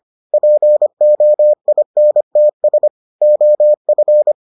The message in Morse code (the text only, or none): POINTS OF